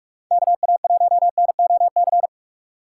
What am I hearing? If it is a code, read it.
Morse code, heard as a keyed tone, about 55 wpm, 700 Hz. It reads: KA1NOC
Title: Speed Racing - USA 2x3 Call Signs 45wpm